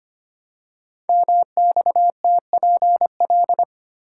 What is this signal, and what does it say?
Morse code, heard as a keyed tone, about 25 wpm, 700 Hz. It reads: MXTPL
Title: Code Groups - 5 Letters - 25wpm